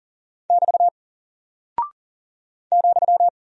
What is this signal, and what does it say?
Morse code, heard as a keyed tone, about 39 wpm, 700 Hz. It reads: <BT> ,
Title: Single Character (Letter:Number:Punctuation:Prosign) 40wpm